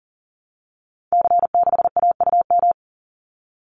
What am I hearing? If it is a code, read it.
Morse code, heard as a keyed tone, about 37 wpm, 700 Hz. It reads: C6AUM
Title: Call Signs Repeated 3x - 40wpm